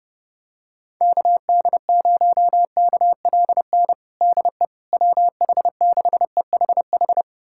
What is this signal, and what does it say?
Morse code, heard as a keyed tone, about 30 wpm, 700 Hz. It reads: KD0KLN DE WH6EHH
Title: USA Callsign DE Callsign - 6 Characters - 30wpm